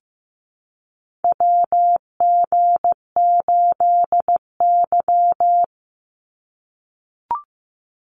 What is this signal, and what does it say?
Morse code, heard as a keyed tone, about 15 wpm, 700 Hz. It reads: WG8Y
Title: NA SOTA Chaser Callsigns 15wpm